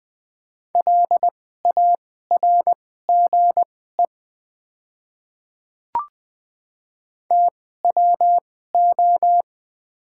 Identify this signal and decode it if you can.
Morse code, heard as a keyed tone, about 20 wpm, spaced as for 14 wpm, 700 Hz. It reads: LARGE TWO
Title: 2x Character Spacing - Top 500 Words - 20wpm